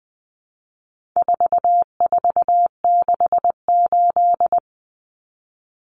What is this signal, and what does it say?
Morse code, heard as a keyed tone, about 20 wpm, 700 Hz. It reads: 4468
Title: Numbers - 4-Digit 20wpm